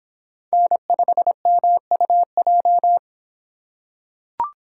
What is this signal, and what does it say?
Morse code, heard as a keyed tone, about 25 wpm, 700 Hz. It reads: N5MUJ